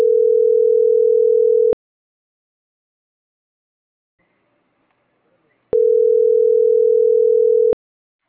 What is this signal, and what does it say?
Ringback tone (North America), 440 + 480 Hz.